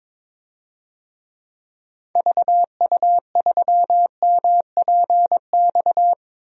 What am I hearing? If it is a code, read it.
Morse code, heard as a keyed tone, about 22 wpm, 700 Hz. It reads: VU3MPX
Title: DXCC-100 Call Signs - 6 Characters - Repeated 2x - 22wpm